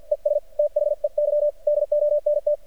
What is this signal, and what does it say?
Morse code, heard as a keyed tone, about 50 wpm, spaced as for 44 wpm, 590 Hz. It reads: ES THEY DONT
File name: archived / CRSnet